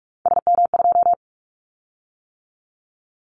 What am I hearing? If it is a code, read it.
Morse code, heard as a keyed tone, about 46 wpm, 700 Hz. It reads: SM2
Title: Callsign Prefixes for Top 25 DXCC - 45wpm